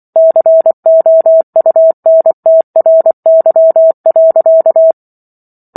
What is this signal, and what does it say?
Morse code, heard as a keyed tone, about 24 wpm, 650 Hz. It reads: COUNTRY.
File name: Chapter 15 - 24 WPM